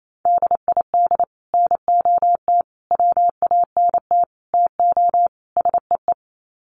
Morse code, heard as a keyed tone, about 28 wpm, 700 Hz. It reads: DID NOT WANT TO SEE
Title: Sets of 5 Words Encore2 28wpm